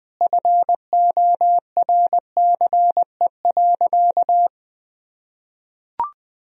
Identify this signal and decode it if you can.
Morse code, heard as a keyed tone, about 20 wpm, 700 Hz. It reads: FORCE.
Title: Sentences from Top 1000 Words - Review Part 4 - 20wpm